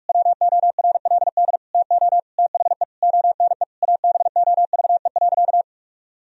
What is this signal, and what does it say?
Morse code, heard as a keyed tone, about 46 wpm, 700 Hz. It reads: WORLD TO THE ONE ABOVE.
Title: Sentences from Top 1000 Words - Review Part 2 - 45wpm